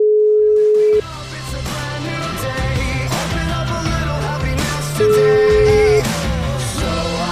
Ringback tone (Europe, 425 Hz), with noise.